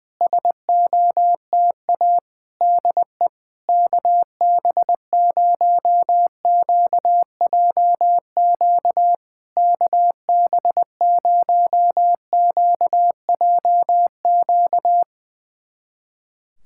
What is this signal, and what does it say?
Morse code, heard as a keyed tone, about 20 wpm, 700 Hz. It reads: SOTA DE KB0QJQ KB0QJQ
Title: SOTA QSO Practice 20wpm